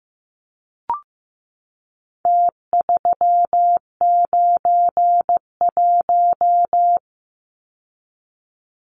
Morse code, heard as a keyed tone, about 15 wpm, 700 Hz. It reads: T391
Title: Numbers - Contest Style - 4-Digit 15wpm